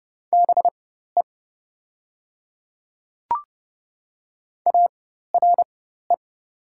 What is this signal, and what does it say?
Morse code, heard as a keyed tone, about 30 wpm, spaced as for 14 wpm, 700 Hz. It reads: BE ARE
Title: 4x Character Spacing - Top 500 Words - 30wpm